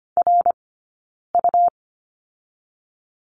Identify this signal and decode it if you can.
Morse code, heard as a keyed tone, about 25 wpm, 700 Hz. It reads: R U
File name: Lesson 17d - Letter u - Two Characters 25wpm